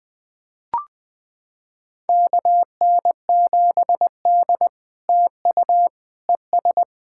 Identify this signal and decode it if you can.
Morse code, heard as a keyed tone, about 20 wpm, 700 Hz. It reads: KN7D TU ES
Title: POTA QSO Practice - 20wpm